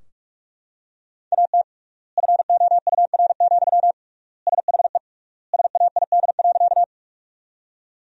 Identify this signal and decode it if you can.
Morse code, heard as a keyed tone, about 42 wpm, 700 Hz. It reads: AT FOUR, SHE SAID.
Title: 3x Word-Spacing - Sentences from Top 300 Words - 45wpm